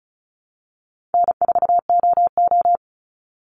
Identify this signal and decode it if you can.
Morse code, heard as a keyed tone, about 34 wpm, 700 Hz. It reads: N4OO
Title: Call Signs - 35wpm